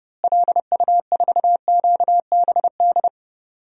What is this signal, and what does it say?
Morse code, heard as a keyed tone, about 30 wpm, 700 Hz. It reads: LU4QBD
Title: DXCC-100 Call Signs - 6 Characters - Repeated 3x - 30wpm